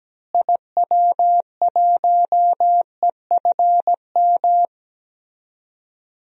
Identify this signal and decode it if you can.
Morse code, heard as a keyed tone, about 17 wpm, 700 Hz. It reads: IW1EFM
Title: DXCC-100 Call Signs - 6 Characters - Repeated 2x - 17wpm